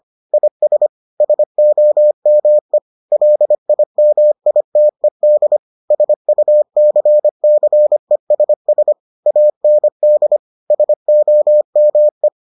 Morse code, heard as a keyed tone, about 25 wpm, 600 Hz. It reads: IS SOME LIMITED SUCCESS AND SOME